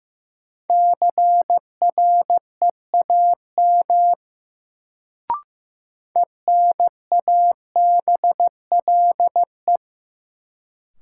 Morse code, heard as a keyed tone, about 15 wpm, 700 Hz. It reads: CREAM ENABLE
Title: Sentences from Top 1500 Words - New Word Drill - 15wpm